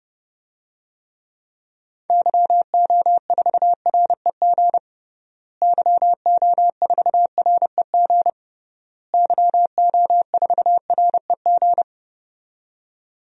Morse code, heard as a keyed tone, about 30 wpm, 700 Hz. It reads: YO4REG YO4REG YO4REG
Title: DXCC-100 Call Signs - 6 Characters - Repeated 3x - 30wpm